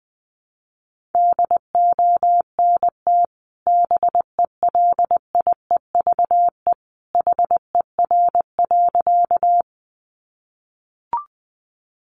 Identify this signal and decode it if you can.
Morse code, heard as a keyed tone, about 20 wpm, 700 Hz. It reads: DONT BELIEVE HER.
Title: Sentences from Top 1000 Words 20wpm